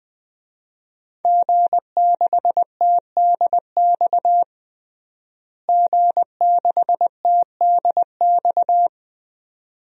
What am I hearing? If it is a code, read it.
Morse code, heard as a keyed tone, about 20 wpm, 700 Hz. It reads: G6TDX G6TDX
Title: DXCC-100 Call Signs - 5 Characters - Repeated 2x - 20wpm